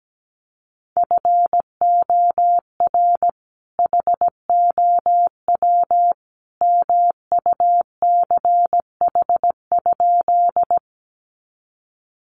Morse code, heard as a keyed tone, about 17 wpm, 700 Hz. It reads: FOR HOW MUCH?
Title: Sentences from Top 200 Words 17wpm